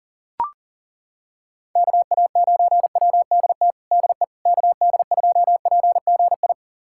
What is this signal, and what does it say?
Morse code, heard as a keyed tone, about 40 wpm, 700 Hz. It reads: KA9WDT DE KD2PGI
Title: USA Callsign DE Callsign - 6 Characters - 40wpm